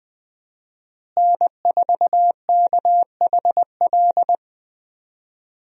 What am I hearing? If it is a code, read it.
Morse code, heard as a keyed tone, about 20 wpm, 700 Hz. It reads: N4KHL
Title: USA Call Signs - 5 Characters - Repeated 2x - 20wpm